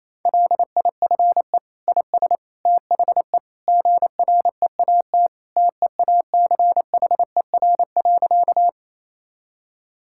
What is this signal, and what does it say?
Morse code, heard as a keyed tone, about 27 wpm, 700 Hz. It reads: LIFE IS THE GREAT TEACHER.
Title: Sentences from Top 1000 Words - Review Part 6 - 28wpm